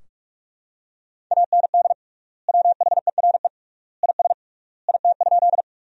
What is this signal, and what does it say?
Morse code, heard as a keyed tone, about 46 wpm, 700 Hz. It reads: AND WHERE IS IT?
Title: 3x Word-Spacing - Sentences from Top 200 Words - 45wpm